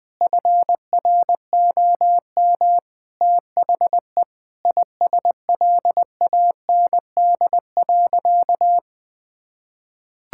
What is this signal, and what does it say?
Morse code, heard as a keyed tone, about 20 wpm, 700 Hz. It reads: FROM THE ISLAND.
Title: Sentences from Top 1000 Words - Review Part 5 - 20wpm